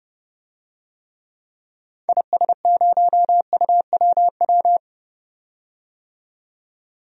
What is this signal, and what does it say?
Morse code, heard as a keyed tone, about 32 wpm, 700 Hz. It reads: IS0UWW
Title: DXCC-100 Call Signs - 6 Characters - 30wpm